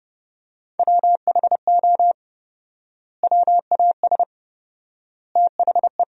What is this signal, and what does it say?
Morse code, heard as a keyed tone, about 31 wpm, 700 Hz. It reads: WHO WAS THE
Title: 4x Word-Spacing - Sentences from Top 300 Words - 30wpm